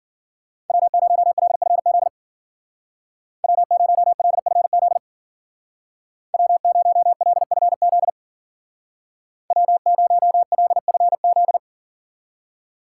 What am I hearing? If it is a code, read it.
Morse code, heard as a keyed tone, about 52 wpm, 700 Hz. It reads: W0LFZ W0LFZ W0LFZ W0LFZ